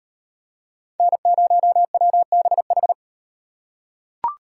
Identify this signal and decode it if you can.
Morse code, heard as a keyed tone, about 38 wpm, 700 Hz. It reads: N0WBH